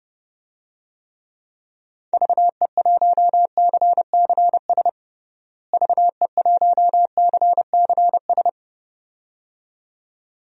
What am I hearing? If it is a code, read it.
Morse code, heard as a keyed tone, about 30 wpm, 700 Hz. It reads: VE1CCS VE1CCS
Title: DXCC-100 Call Signs - 6 Characters - Repeated 2x - 30wpm